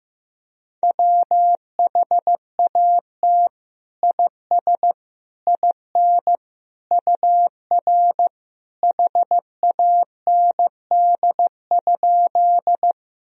Morse code, heard as a keyed tone, about 15 wpm, 700 Hz. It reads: WHAT IS IN UR HAND?
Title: Sentences from Top 200 Words and Abbreviations 15wpm